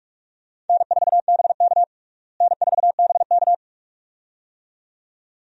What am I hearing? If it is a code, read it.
Morse code, heard as a keyed tone, about 46 wpm, 700 Hz. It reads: N4BK N4BK
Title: USA Call Signs - 4 Characters - Repeated 2x - 45wpm